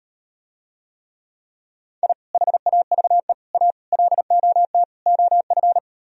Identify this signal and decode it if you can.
Morse code, heard as a keyed tone, about 39 wpm, 700 Hz. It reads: I HAVE A LOT OF